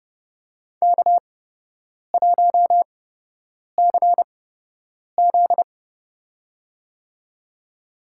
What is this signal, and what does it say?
Morse code, heard as a keyed tone, about 32 wpm, spaced as for 8 wpm, 700 Hz. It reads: K1CZ
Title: USA Call Signs - 8x Character Spacing - 30wpm